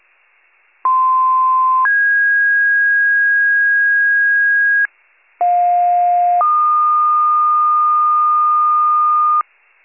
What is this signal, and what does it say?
Two-tone page, heard as a one-second tone, then a three-second tone, about 1040 then 1650 Hz, about 710 then 1220 Hz, with noise. It is heard from 0.8 to 9.4 s.